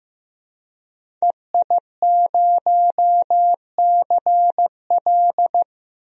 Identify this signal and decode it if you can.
Morse code, heard as a keyed tone, about 15 wpm, 700 Hz. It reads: EI0CL
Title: DXCC-100 Call Signs - 5 Characters - 15wpm